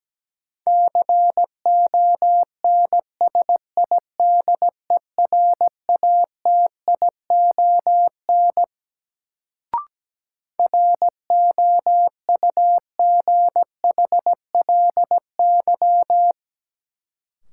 Morse code, heard as a keyed tone, about 17 wpm, 700 Hz. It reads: CONSIDERATION ROUGHLY
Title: Sentences from Top 1500 Words - New Word Drill - 17wpm